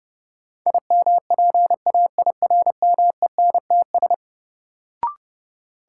Morse code, heard as a keyed tone, about 30 wpm, 700 Hz. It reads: IMPAIRMENTS
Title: Sets of 5 Words Encore - New Word Drill - 30wpm